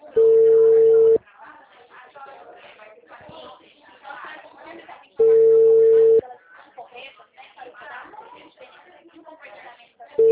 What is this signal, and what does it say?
Ringback tone (China, 450 Hz), with speech.